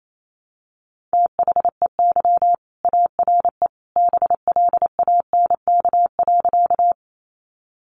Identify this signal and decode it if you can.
Morse code, heard as a keyed tone, about 28 wpm, 700 Hz. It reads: THEY ARE BLANK.
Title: Sentences from Top 1500 Words 28wpm